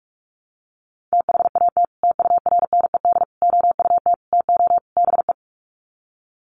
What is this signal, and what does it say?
Morse code, heard as a keyed tone, about 44 wpm, 700 Hz. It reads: THAT TURNED OUT TO BE
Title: Sets of 5 Words Encore2 45wpm